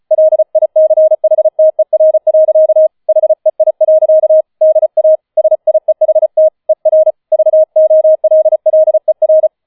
Morse code, heard as a keyed tone, about 35 wpm, 630 Hz. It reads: LICHTER. HEI. DA SIEHT ER VOLLER